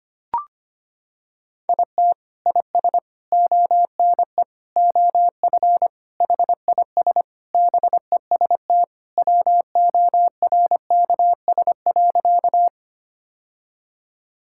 Morse code, heard as a keyed tone, about 25 wpm, 700 Hz. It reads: IT IS ONE OF HIS BEST WORKS.
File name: practice / 25wpm / Sentences with 3-Word Phrases - 25wpm